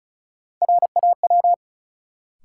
Morse code, heard as a keyed tone, about 36 wpm, 700 Hz. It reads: RAW
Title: Sentences from Top 1500 Words - New Word Drill - 35wpm